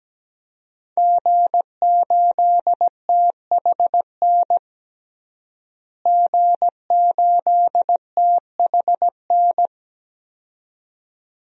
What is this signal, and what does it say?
Morse code, heard as a keyed tone, about 17 wpm, 700 Hz. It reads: G8THN G8THN
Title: DXCC-100 Call Signs - 5 Characters - Repeated 2x - 17wpm